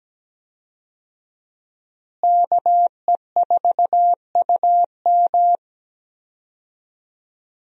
Morse code, heard as a keyed tone, about 17 wpm, 700 Hz. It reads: KE4UM
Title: USA Call Signs - 5 Characters - Repeated 2x - 17wpm